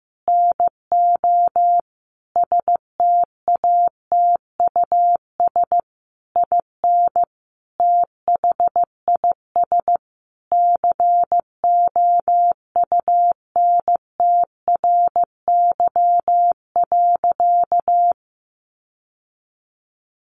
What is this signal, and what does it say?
Morse code, heard as a keyed tone, about 15 wpm, 700 Hz. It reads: NO STATUS IN THIS COUNTRY.
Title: Sentences from Top 1400 Words 15wpm